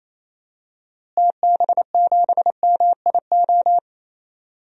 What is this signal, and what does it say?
Morse code, heard as a keyed tone, about 28 wpm, 700 Hz. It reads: TB7MIO
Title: DXCC-100 Call Signs - 6 Characters - Repeated 2x - 28wpm